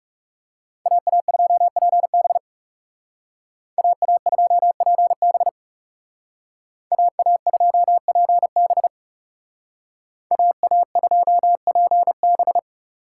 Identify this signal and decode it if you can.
Morse code, heard as a keyed tone, about 36 wpm, 700 Hz. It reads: AA2PB AA2PB AA2PB AA2PB
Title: Speed Racing - USA 2x2 Call Signs 30wpm